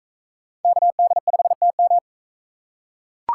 Morse code, heard as a keyed tone, about 39 wpm, 700 Hz. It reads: KD5TM